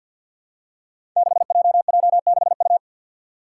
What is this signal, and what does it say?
Morse code, heard as a keyed tone, about 47 wpm, 700 Hz. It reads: 6JJ6U
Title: Five Challenging Letter-Number Combinations 50wpm